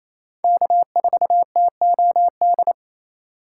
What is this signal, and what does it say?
Morse code, heard as a keyed tone, about 28 wpm, 700 Hz. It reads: K4TOD
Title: USA Call Signs - 5 Characters - 2x Character Spacing - 28wpm